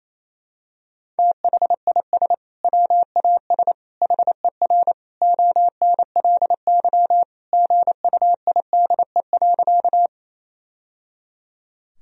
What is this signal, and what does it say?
Morse code, heard as a keyed tone, about 28 wpm, 700 Hz. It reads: THIS WAS HER ONLY GUIDE.
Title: Sentences from Top 1000 Words - Review Part 3 - 28wpm